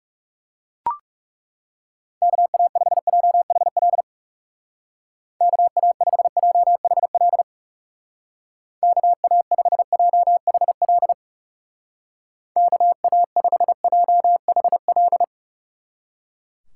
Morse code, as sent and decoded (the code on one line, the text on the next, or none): -.- .- ..... .--- .... .-.. / -.- .- ..... .--- .... .-.. / -.- .- ..... .--- .... .-.. / -.- .- ..... .--- .... .-..
KA5JHL KA5JHL KA5JHL KA5JHL